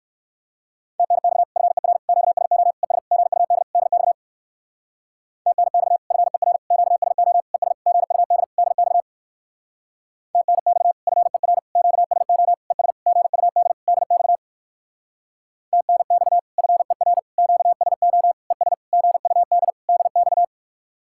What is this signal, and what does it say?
Morse code, heard as a keyed tone, about 58 wpm, 700 Hz. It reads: TNX FER QSO ES GUD DX TNX FER QSO ES GUD DX TNX FER QSO ES GUD DX TNX FER QSO ES GUD DX